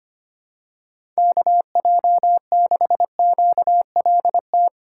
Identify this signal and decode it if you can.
Morse code, heard as a keyed tone, about 25 wpm, 700 Hz. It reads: KJ6QLT
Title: USA Call Signs - 6 Characters - 2x Character Spacing - 25wpm